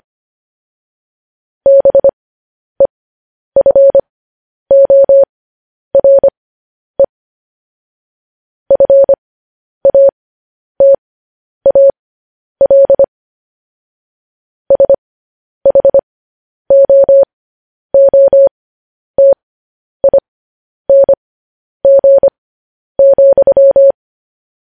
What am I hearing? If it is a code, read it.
Morse code, heard as a keyed tone, about 25 wpm, spaced as for 10 wpm, 550 Hz. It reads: BEFORE FATAL SHOOTING,